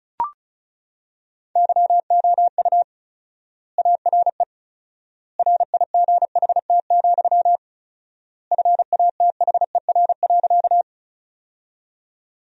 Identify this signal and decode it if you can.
Morse code, heard as a keyed tone, about 34 wpm, 700 Hz. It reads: YOU ARE RIGHT, FATHER.
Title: 4x Word-Spacing - Sentences from Top 300 Words - 35wpm